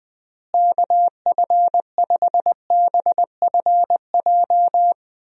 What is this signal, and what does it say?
Morse code, heard as a keyed tone, about 20 wpm, 700 Hz. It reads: KF5BFJ
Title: USA Call Signs - 6 Characters - Repeated 3x - 20wpm